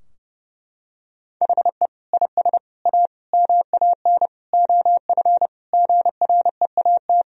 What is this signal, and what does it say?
Morse code, heard as a keyed tone, about 30 wpm, 700 Hz. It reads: HE IS A MAN OF GREAT
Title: Sentences from Top 1000 Words - Review Part 3 - 30wpm